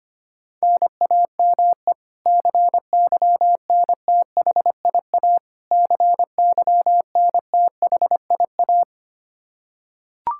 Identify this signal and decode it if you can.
Morse code, heard as a keyed tone, about 25 wpm, 700 Hz. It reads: NAME CYNTHIA CYNTHIA
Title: K1USN SST QSO Practice 25wpm